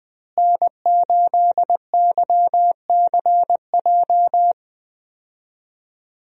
Morse code, heard as a keyed tone, about 20 wpm, 700 Hz. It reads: N8YCJ
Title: USA Call Signs - 5 Characters - Repeated 3x - 20wpm